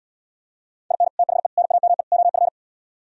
Morse code, heard as a keyed tone, about 77 wpm, 700 Hz. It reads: IN THE NAME OF